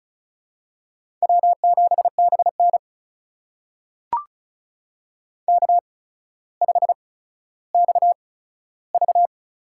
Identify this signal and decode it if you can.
Morse code, heard as a keyed tone, about 36 wpm, spaced as for 15 wpm, 700 Hz. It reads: W7BN K5XV
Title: USA Call Signs - 8x Character Spacing - 35wpm